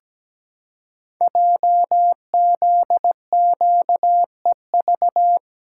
Morse code, heard as a keyed tone, about 17 wpm, 700 Hz. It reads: JZQEV
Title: Code Groups - 5 Letters - 17wpm